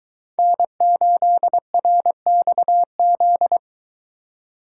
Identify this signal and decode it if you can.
Morse code, heard as a keyed tone, about 22 wpm, 700 Hz. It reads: N8RXZ